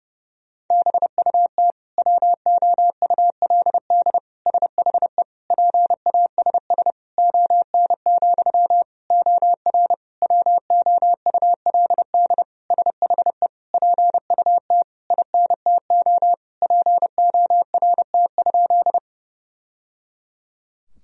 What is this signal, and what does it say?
Morse code, heard as a keyed tone, about 30 wpm, 700 Hz. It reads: BUT WOULD SHE PASS ON, OR WOULD SHE PUT INTO PORT?
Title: Sentences from Top 1000 Words - Review Part 1 - 30wpm